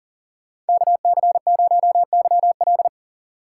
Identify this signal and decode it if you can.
Morse code, heard as a keyed tone, about 52 wpm, 700 Hz. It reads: KC0YL